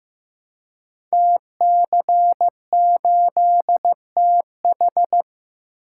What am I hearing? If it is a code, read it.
Morse code, heard as a keyed tone, about 15 wpm, 700 Hz. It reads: TC8TH
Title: Code Groups - 5 Letter:Numbers - 15wpm